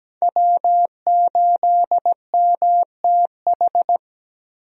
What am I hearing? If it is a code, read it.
Morse code, heard as a keyed tone, about 17 wpm, 700 Hz. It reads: W8MTH
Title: USA Call Signs - 5 Characters - Repeated 2x - 17wpm